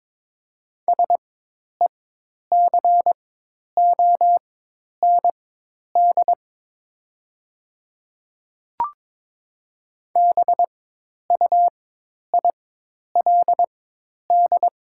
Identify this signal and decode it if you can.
Morse code, heard as a keyed tone, about 22 wpm, spaced as for 10 wpm, 700 Hz. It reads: SECOND BUILD
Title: 4x Character Spacing - Top 300 Words - 22wpm